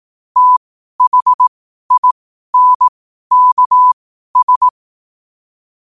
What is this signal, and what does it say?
Morse code, heard as a keyed tone, about 18 wpm, spaced as for 12 wpm, 1000 Hz. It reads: THINKS
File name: qotdfiles / 12wpm